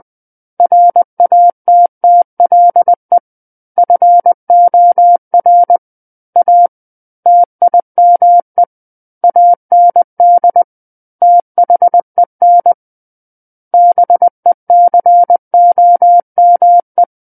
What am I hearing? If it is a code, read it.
Morse code, heard as a keyed tone, about 20 wpm, 700 Hz. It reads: RATTLE FOR A TIME AND THEN BECOME
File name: 20 WPM